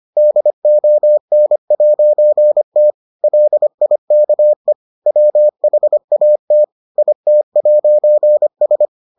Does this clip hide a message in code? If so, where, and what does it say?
Yes: Morse code from 0.2 to 8.9 s, heard as a keyed tone, about 25 wpm, 600 Hz. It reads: DON'T LIKE WHAT IT'S